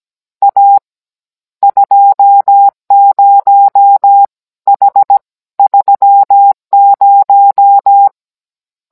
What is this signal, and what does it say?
Morse code, heard as a keyed tone, about 17 wpm, 800 Hz. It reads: A 20 H 30